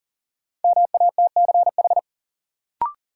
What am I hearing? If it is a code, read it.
Morse code, heard as a keyed tone, about 40 wpm, 700 Hz. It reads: MATCH
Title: Sentences from Top 1600 Words - New Word Drill - 40wpm